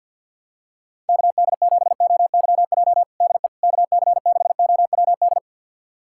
Morse code, heard as a keyed tone, about 51 wpm, 700 Hz. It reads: KD7OYJ DE KC6OWD
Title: USA Callsign DE Callsign - 6 Characters - 50wpm